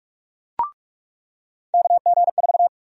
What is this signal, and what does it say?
Morse code, heard as a keyed tone, about 46 wpm, 700 Hz. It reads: KG4
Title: USA Call Sign Prefixes - 45wpm